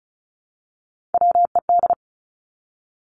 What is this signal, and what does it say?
Morse code, heard as a keyed tone, about 36 wpm, 700 Hz. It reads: WED
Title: Three Challenging Letter-Number Combinations 35wpm